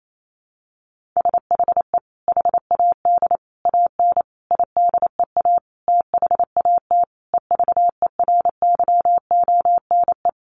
Morse code, heard as a keyed tone, about 28 wpm, 700 Hz. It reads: SHE HAD AN IDEA THAT EVERYONE
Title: Sentences from Top 1400 Words 28wpm